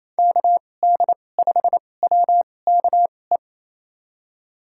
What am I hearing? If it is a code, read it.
Morse code, heard as a keyed tone, about 27 wpm, spaced as for 20 wpm, 700 Hz. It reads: KD5WKE
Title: USA Call Signs - 2x Character Spacing - 28wpm